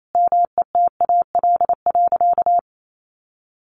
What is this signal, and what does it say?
Morse code, heard as a keyed tone, about 28 wpm, 700 Hz. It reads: METAL.
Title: A Sound Of Thunder 28wpm